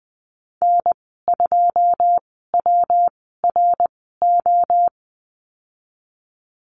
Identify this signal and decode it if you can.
Morse code, heard as a keyed tone, about 20 wpm, spaced as for 15 wpm, 700 Hz. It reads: N2WRO
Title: 128 Lesson 21g - Number 6 - Call Signs 20wpm